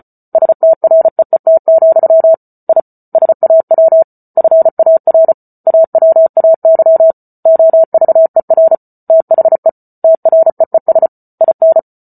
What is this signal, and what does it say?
Morse code, heard as a keyed tone, about 34 wpm, 650 Hz. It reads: STREET, I SAW FAR AWAY OVER THE TREES IN